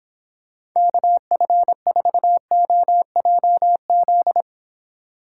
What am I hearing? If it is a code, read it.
Morse code, heard as a keyed tone, about 25 wpm, 700 Hz. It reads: KF4OJZ